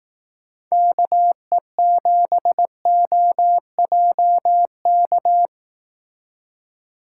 Morse code, heard as a keyed tone, about 18 wpm, 700 Hz. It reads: KE7OJK
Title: Speed Racing - USA 2x3 Call Signs 15wpm